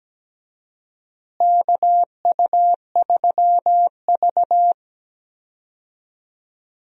Morse code, heard as a keyed tone, about 17 wpm, 700 Hz. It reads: KU3V
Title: USA Call Signs - 4 Characters - Repeated 2x - 17wpm